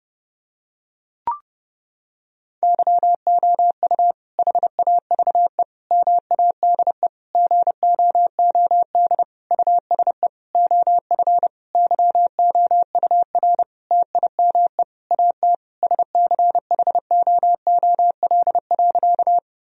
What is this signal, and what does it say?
Morse code, heard as a keyed tone, about 32 wpm, 700 Hz. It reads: YOU HAVE MADE GOOD USE OF YOUR TIME AT SCHOOL.
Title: Sentences from Top 1000 Words - Review Part 1 - 30wpm